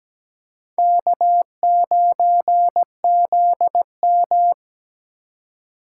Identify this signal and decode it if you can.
Morse code, heard as a keyed tone, about 17 wpm, 700 Hz. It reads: K9ZM
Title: USA Call Signs - 4 Characters - 4x Character Spacing - 17wpm